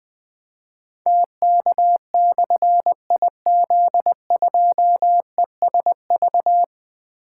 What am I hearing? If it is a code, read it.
Morse code, heard as a keyed tone, about 20 wpm, 700 Hz. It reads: TK/IZ2ESV
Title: Difficult Call Signs 20wpm